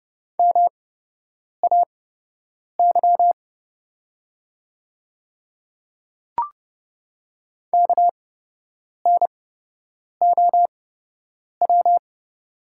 Morse code, heard as a keyed tone, about 29 wpm, spaced as for 8 wpm, 700 Hz. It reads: MAY KNOW
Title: 8x Character Spacing - Top 100 Words - 30wpm